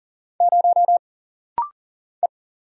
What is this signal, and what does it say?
Morse code, heard as a keyed tone, about 40 wpm, 700 Hz. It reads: 0 E
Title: Single Letter-Number - Rapid-Fire 40wpm